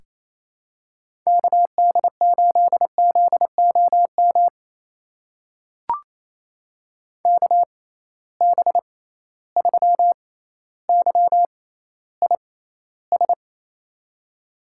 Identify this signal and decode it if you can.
Morse code, heard as a keyed tone, about 28 wpm, spaced as for 14 wpm, 700 Hz. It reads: KD8ZOM KB3YIS
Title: USA Call Signs - 6 Characters - 6x Character Spacing - 28wpm